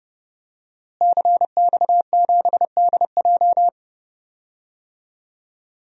Morse code, heard as a keyed tone, about 30 wpm, 700 Hz. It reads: CX7DJ
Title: DXCC-100 Call Signs - 5 Characters - Repeated 3x - 30wpm